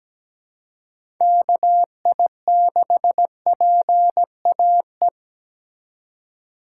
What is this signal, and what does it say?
Morse code, heard as a keyed tone, about 17 wpm, 700 Hz. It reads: KI6PAE
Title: USA Call Signs - 6 Characters - 3x Character Spacing - 17wpm